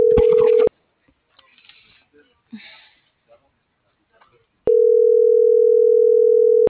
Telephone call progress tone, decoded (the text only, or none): ringback tone